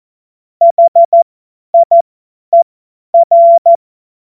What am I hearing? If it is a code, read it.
Morse code, heard as a keyed tone, about 14 wpm, spaced as for 10 wpm, 680 Hz. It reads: HIER
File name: Witz-07-Oktober-2025-50-F70